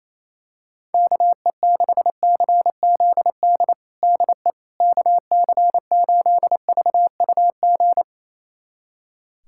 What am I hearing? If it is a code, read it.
Morse code, heard as a keyed tone, about 28 wpm, 700 Hz. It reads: KE6CZD DE KC8VUG
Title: USA Callsign DE Callsign - 6 Characters - 28wpm